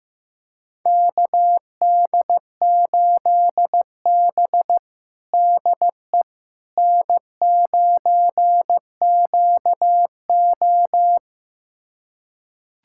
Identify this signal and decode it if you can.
Morse code, heard as a keyed tone, about 15 wpm, 700 Hz. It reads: KD8B DE N9QO
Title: USA Callsign DE Callsign - 4 Characters - 15wpm